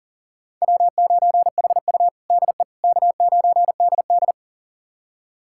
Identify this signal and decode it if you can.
Morse code, heard as a keyed tone, about 40 wpm, 700 Hz. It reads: W9HU DE K9DD
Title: USA Callsign DE Callsign - 4 Characters - 40wpm